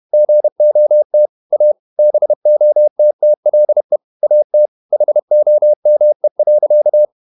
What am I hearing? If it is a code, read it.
Morse code, heard as a keyed tone, about 31 wpm, 600 Hz. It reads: GOT A BOTTLE AT HOME.
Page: Never 31wpm | CW med Gnister